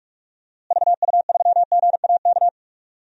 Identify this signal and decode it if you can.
Morse code, heard as a keyed tone, about 46 wpm, 700 Hz. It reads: VU3GAK
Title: DXCC-100 Call Signs - 6 Characters - Repeated 2x - 45wpm